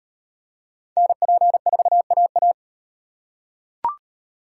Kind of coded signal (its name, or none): Morse code